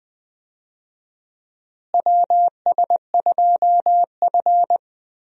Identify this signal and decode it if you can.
Morse code, heard as a keyed tone, about 21 wpm, 700 Hz. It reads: WS2F
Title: USA Call Signs - 4 Characters - Repeated 3x - 20wpm